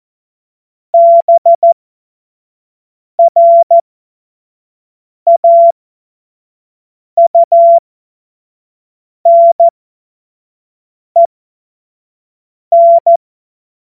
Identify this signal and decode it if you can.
Morse code, heard as a keyed tone, about 14 wpm, spaced as for 5 wpm, 680 Hz. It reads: BRAUNEN
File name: Morsekurs / Witze / Witz 21. Februar 2023 25 F70